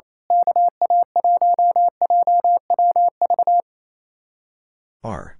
Morse code, heard as a keyed tone, about 29 wpm, 700 Hz. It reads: KA1JWV
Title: Speed-Racing Common QSO Phrases - 28wpm